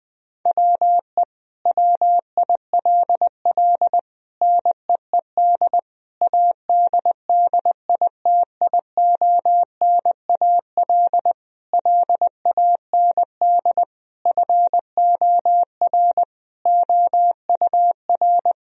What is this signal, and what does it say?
Morse code, heard as a keyed tone, about 20 wpm, 700 Hz. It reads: WE WILL NEED ADDITIONAL LAND FOR OUR
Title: Sentences from Top 1000 Words - Review Part 4 - 20wpm